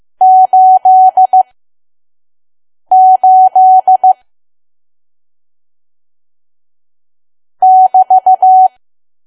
Morse code, heard as a keyed tone, about 15 wpm, spaced as for 5 wpm, 750 Hz. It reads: 88 <BT>
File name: Archive / 5 WPM files / PM